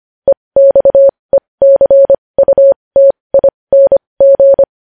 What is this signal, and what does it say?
Morse code, heard as a keyed tone, about 25 wpm, spaced as for 20 wpm, 550 Hz. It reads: EXECUTING
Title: News Headlines at 20